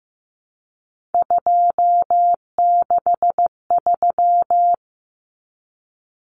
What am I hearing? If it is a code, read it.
Morse code, heard as a keyed tone, about 15 wpm, 700 Hz. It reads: 263
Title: Numbers - 3-Digit 15wpm